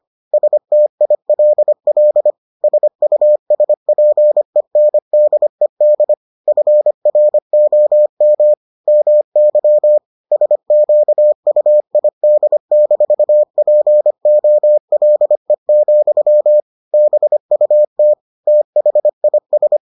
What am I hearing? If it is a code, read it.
Morse code, heard as a keyed tone, about 25 wpm, 600 Hz. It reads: STILL SUSPENDED FROM MY SQUID-POLE, BUT THIS